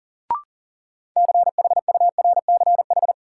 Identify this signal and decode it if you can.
Morse code, heard as a keyed tone, about 40 wpm, 700 Hz. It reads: CHURCH
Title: Sets of 5 Words - New Word Drill - 40wpm